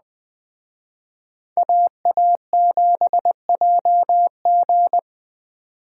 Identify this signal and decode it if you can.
Morse code, heard as a keyed tone, about 20 wpm, 700 Hz. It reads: AA7JG
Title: Speed Racing - USA 2x2 Call Signs 17wpm